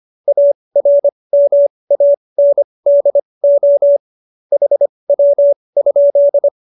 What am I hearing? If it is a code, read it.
Morse code, heard as a keyed tone, about 25 wpm, spaced as for 20 wpm, 570 Hz. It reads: ARMANDO HW?
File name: qso215b 20